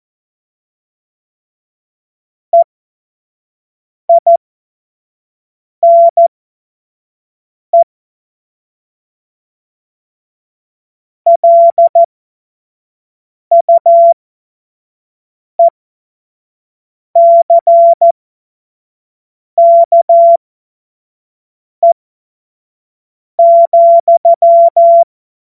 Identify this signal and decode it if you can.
Morse code, heard as a keyed tone, about 14 wpm, spaced as for 5 wpm, 680 Hz. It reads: EINE LUECKE,
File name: Witz-07-Oktober-2025-25-F70